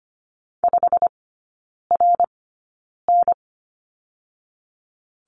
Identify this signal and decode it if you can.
Morse code, heard as a keyed tone, about 25 wpm, spaced as for 9 wpm, 700 Hz. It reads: 5RN
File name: Lesson 15e - Number 5 - Three Characters 25wpm